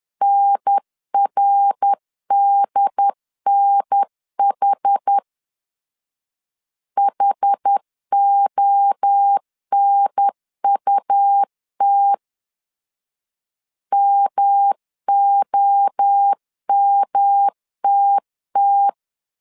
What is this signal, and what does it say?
Morse code, heard as a keyed tone, about 10 wpm, 800 Hz. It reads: NRDNH HONUT MOMTT
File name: 10wpm